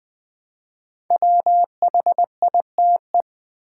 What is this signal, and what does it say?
Morse code, heard as a keyed tone, about 20 wpm, 700 Hz. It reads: WHITE
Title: Build A Word - 1 Letter At A Time - 5-Letter Words - 20wpm